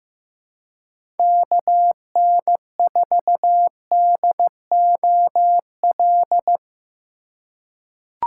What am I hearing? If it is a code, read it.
Morse code, heard as a keyed tone, about 16 wpm, 700 Hz. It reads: KN4DOL
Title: USA Call Signs - 6 Characters - 2x Character Spacing - 15wpm